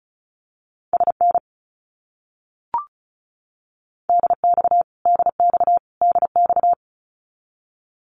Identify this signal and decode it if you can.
Morse code, heard as a keyed tone, about 34 wpm, 700 Hz. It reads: SN DX DX DX
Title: Abbreviations Repeated 3x - 35wpm